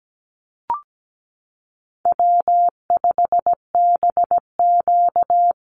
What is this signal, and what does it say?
Morse code, heard as a keyed tone, about 17 wpm, 700 Hz. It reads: W5BQ
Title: Easy Call Signs - 17wpm